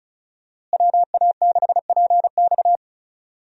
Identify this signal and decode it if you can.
Morse code, heard as a keyed tone, about 36 wpm, 700 Hz. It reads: WA6PX
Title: USA Call Signs - 5 Characters - Repeated 3x - 35wpm